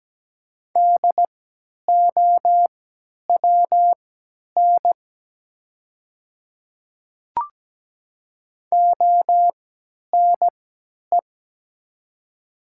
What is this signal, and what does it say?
Morse code, heard as a keyed tone, about 17 wpm, spaced as for 10 wpm, 700 Hz. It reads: DOWN ONE
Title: 3x Character Spacing - Top 100 Words - 17wpm